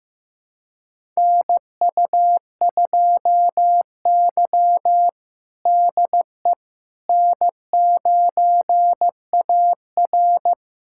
Morse code, heard as a keyed tone, about 15 wpm, 700 Hz. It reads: NU2Y DE N9AR
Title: USA Callsign DE Callsign - 4 Characters - 15wpm